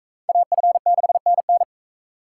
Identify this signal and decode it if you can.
Morse code, heard as a keyed tone, about 39 wpm, 700 Hz. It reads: AF6NN